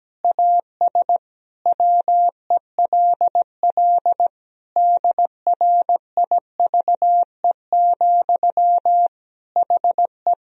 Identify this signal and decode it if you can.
Morse code, heard as a keyed tone, about 17 wpm, 700 Hz. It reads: AS WELL DRIVE, HE
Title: Sentences from Top 1000 Words - Review Part 2 - 17wpm